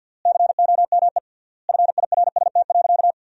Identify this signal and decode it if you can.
Morse code, heard as a keyed tone, about 50 wpm, 700 Hz. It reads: COME FIRST.
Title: 3x Word-Spacing - Sentences from Top 300 Words - 50wpm